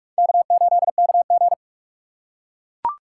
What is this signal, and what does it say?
Morse code, heard as a keyed tone, about 44 wpm, 700 Hz. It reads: K8KG
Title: USA Call Signs - 4 Characters - 2x Character Spacing - 45wpm